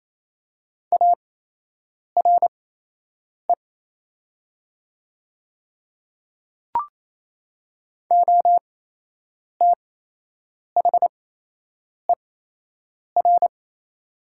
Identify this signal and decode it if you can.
Morse code, heard as a keyed tone, about 27 wpm, spaced as for 8 wpm, 700 Hz. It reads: ARE OTHER
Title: 8x Character Spacing - Top 100 Words - 28wpm